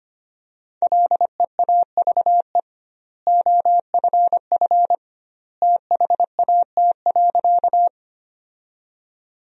Morse code, heard as a keyed tone, about 25 wpm, 700 Hz. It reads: LEAVE OFF THAT.
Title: 2x Word-Spacing - Sentences from Top 400 Words - 25wpm